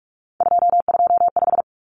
Morse code, heard as a keyed tone, about 44 wpm, 700 Hz. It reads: 225
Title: Common Cut Numbers - 3-Digit 45wpm